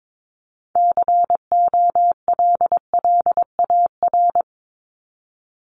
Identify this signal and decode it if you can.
Morse code, heard as a keyed tone, about 22 wpm, 700 Hz. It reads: COLLAR
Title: Top 1751-2000 Words 22wpm